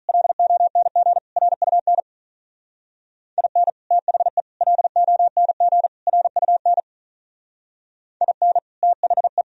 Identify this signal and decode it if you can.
Morse code, heard as a keyed tone, about 44 wpm, 700 Hz. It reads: LONG RUN IN THE LONG RUN IN THE